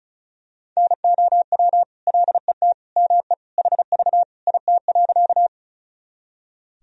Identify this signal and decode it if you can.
Morse code, heard as a keyed tone, about 34 wpm, 700 Hz. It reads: NOW LET ME HV IT.
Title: Sentences from Top 300 Words and Abbreviations 35wpm